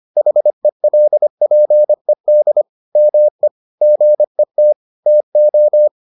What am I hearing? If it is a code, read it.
Morse code, heard as a keyed tone, about 25 wpm, 600 Hz. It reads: HELPED ME GET TO